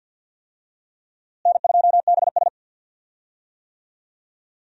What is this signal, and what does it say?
Morse code, heard as a keyed tone, about 47 wpm, 700 Hz. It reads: N2BS
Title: USA Call Signs - 4 Characters - Repeated 3x - 50wpm